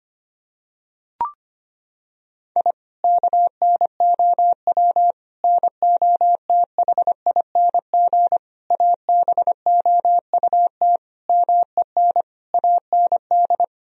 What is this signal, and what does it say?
Morse code, heard as a keyed tone, about 25 wpm, 700 Hz. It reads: I KNOW NOTHING ABOUT MEN AND
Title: Sentences from Top 1000 Words - Review Part 2 - 25wpm